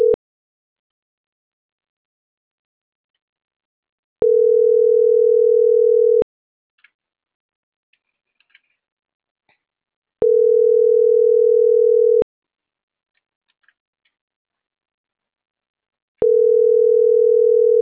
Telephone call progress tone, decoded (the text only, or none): ringback tone